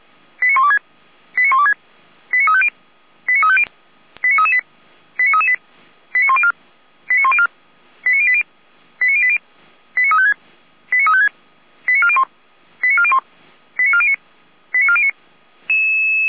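Five-tone selective call (ZVEI), heard as quick runs of five tones, tones of about 70 ms. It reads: 89126 89126 89350 89350 89338 89338 89104 89104 89080 89080 89356 89356 89401 89401 89409 89409
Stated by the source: ZVEI five-tone selective call received by SDR, logged by fire brigade feuerwehr-orlamuende.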